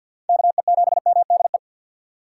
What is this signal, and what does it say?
Morse code, heard as a keyed tone, about 53 wpm, 700 Hz. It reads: KE7MDE
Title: USA Call Signs - 2x Character Spacing - 50wpm